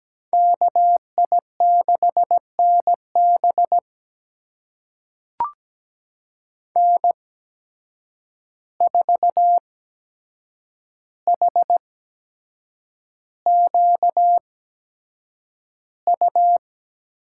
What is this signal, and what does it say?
Morse code, heard as a keyed tone, about 17 wpm, spaced as for 7 wpm, 700 Hz. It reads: KI6NB N4HQU
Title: USA Call Signs - 5 Characters - 8x Character Spacing - 17wpm